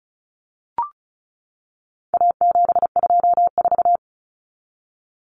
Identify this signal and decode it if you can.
Morse code, heard as a keyed tone, about 34 wpm, 700 Hz. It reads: A724
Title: Common Cut Numbers - 4-Digit 35wpm